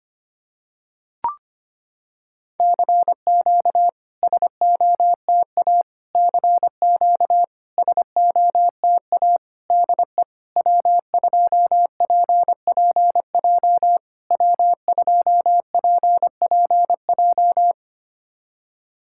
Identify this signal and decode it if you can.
Morse code, heard as a keyed tone, about 25 wpm, 700 Hz. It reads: CQ SOTA CQ SOTA DE W2PPJ W2PPJ
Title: SOTA QSO Practice 25wpm